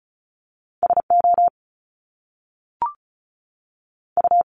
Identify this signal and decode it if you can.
Morse code, heard as a keyed tone, about 36 wpm, 700 Hz. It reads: S O U
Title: Top Two Letter Combinations in Words 35wpm